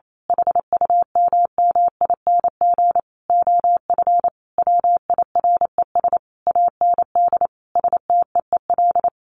Morse code, heard as a keyed tone, about 28 wpm, 700 Hz. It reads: HUMMING OF WIRES AND STEEL
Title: A Sound Of Thunder 28wpm